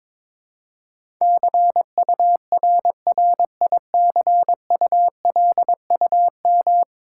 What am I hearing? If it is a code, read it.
Morse code, heard as a keyed tone, about 22 wpm, 700 Hz. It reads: CURRICULUM
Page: morse-code-ninja-practice-22wpm